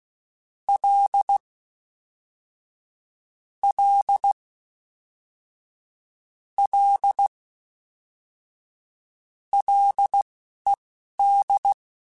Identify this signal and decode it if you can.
Morse code, heard as a keyed tone, about 16 wpm, spaced as for 12 wpm, 800 Hz. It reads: L L L LED
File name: Ls / Single L m3s / Single L Prefixe 3x2 Twice @10